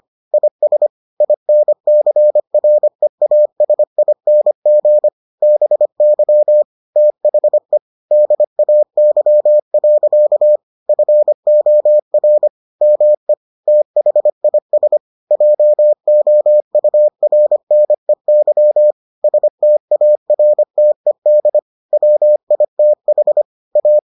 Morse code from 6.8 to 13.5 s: THE DAY. FOR ME

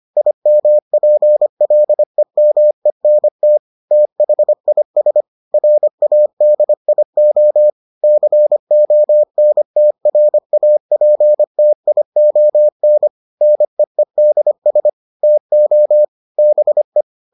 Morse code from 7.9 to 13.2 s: CONTRAPTION

